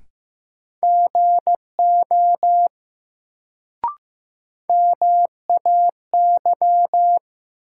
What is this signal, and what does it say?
Morse code, heard as a keyed tone, about 15 wpm, 700 Hz. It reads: GO MAY
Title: Sets of 2 Words - New Word Drill - 15wpm